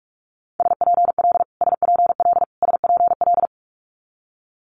Morse code, heard as a keyed tone, about 46 wpm, 700 Hz. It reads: SPL SPL SPL
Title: Multi-letter Phonemes Repeated 3x 45wpm